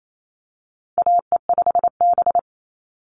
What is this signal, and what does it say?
Morse code, heard as a keyed tone, about 28 wpm, 700 Hz. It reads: AE5B
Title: NA SOTA Chaser Callsigns 28wpm